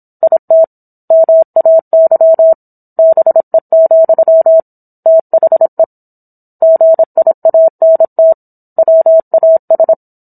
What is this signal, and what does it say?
Morse code, heard as a keyed tone, about 27 wpm, 650 Hz. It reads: IT MAY BE, THE GIANT WAS